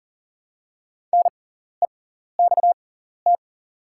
Morse code, heard as a keyed tone, about 40 wpm, spaced as for 14 wpm, 700 Hz. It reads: NEXT
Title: 6x Character Spacing - Top 300 Words - 40wpm